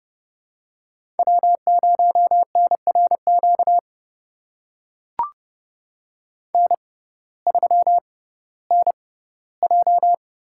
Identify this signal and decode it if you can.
Morse code, heard as a keyed tone, about 30 wpm, spaced as for 17 wpm, 700 Hz. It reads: W0NRQ N3NJ
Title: USA Call Signs - 5 Characters - 6x Character Spacing - 30wpm